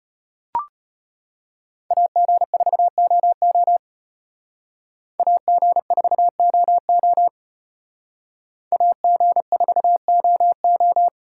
Morse code, heard as a keyed tone, about 32 wpm, 700 Hz. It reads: AG4OO AG4OO AG4OO